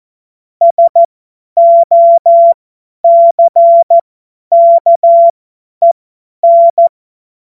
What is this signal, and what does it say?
Morse code, heard as a keyed tone, about 14 wpm, spaced as for 10 wpm, 680 Hz. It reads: SOCKEN
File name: Morsekurs / Witze / Witz 18. Juli 2023 50 F70